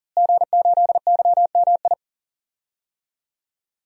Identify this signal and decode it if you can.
Morse code, heard as a keyed tone, about 37 wpm, 700 Hz. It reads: G8YMI
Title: DXCC-100 Call Signs - 5 Characters - Repeated 3x - 40wpm